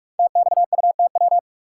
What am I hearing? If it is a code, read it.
Morse code, heard as a keyed tone, about 44 wpm, 690 Hz. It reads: TXUTW
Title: Code Groups - 5 Letters - 45wpm